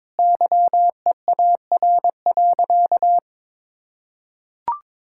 Morse code, heard as a keyed tone, about 22 wpm, 700 Hz. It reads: YEAR.
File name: Sentences with 7-Word Phrases - 22wpm